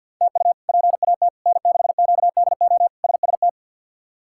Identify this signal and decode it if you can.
Morse code, heard as a keyed tone, about 52 wpm, 700 Hz. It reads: TU PAT N6QDO SST
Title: K1USN SST QSO Practice - 50wpm